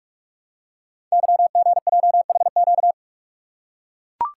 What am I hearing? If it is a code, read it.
Morse code, heard as a keyed tone, about 44 wpm, 700 Hz. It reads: YGJHQ
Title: Code Groups - 5 Letters - 45wpm